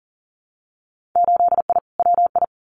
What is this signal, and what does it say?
Morse code, heard as a keyed tone, about 40 wpm, 700 Hz. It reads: 8I WI
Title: Winter Field Day Exchanges 40wpm